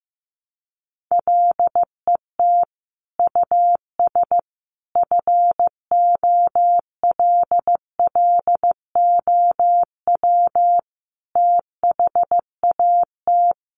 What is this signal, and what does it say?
Morse code, heard as a keyed tone, about 15 wpm, 700 Hz. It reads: LET US FOLLOW THAT
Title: Sentences from Top 1400 Words 15wpm